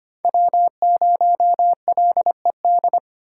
Morse code, heard as a keyed tone, about 32 wpm, 700 Hz. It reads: W0LED